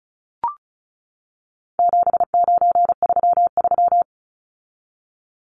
Morse code, heard as a keyed tone, about 34 wpm, 700 Hz. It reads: 7933
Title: Numbers - 4-Digit 35wpm